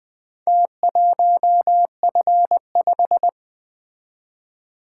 Morse code, heard as a keyed tone, about 24 wpm, 700 Hz. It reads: T1F5
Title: Speed-Racing - Four Letter-Number Combinations - 20wpm